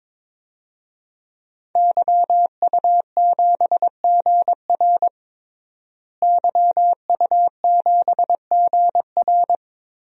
Morse code, heard as a keyed tone, about 22 wpm, 700 Hz. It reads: YU7GR YU7GR
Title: DXCC-100 Call Signs - 5 Characters - Repeated 2x - 22wpm